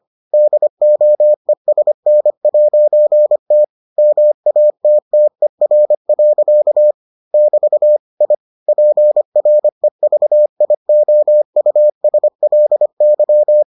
Morse code, heard as a keyed tone, about 25 wpm, 600 Hz. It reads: DOESN'T MATTER. <BT> I PREVIOUSLY